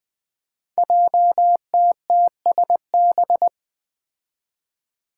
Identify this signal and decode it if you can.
Morse code, heard as a keyed tone, about 20 wpm, 700 Hz. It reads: JTTSB
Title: Five Challenging Letter-Number Combinations 20wpm